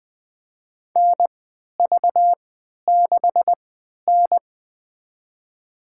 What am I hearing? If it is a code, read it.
Morse code, heard as a keyed tone, about 21 wpm, spaced as for 12 wpm, 700 Hz. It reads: NV6N
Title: USA Call Signs - 3x Character Spacing - 20wpm